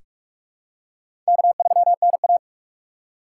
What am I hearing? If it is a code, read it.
Morse code, heard as a keyed tone, about 44 wpm, 700 Hz. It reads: K3NA
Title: USA Call Signs - 4 Characters - 4x Character Spacing - 45wpm